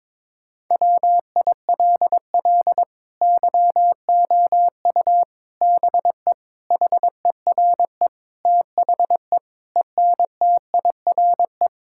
Morse code, heard as a keyed tone, about 23 wpm, 700 Hz. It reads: WILL YOU BE HERE THE ENTIRE
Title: Sentences from Top 1000 Words - Review Part 5 - 22wpm